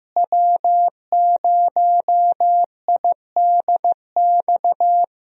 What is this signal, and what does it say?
Morse code, heard as a keyed tone, about 15 wpm, 700 Hz. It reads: W0IDX
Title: USA Call Signs - 5 Characters - Repeated 3x - 15wpm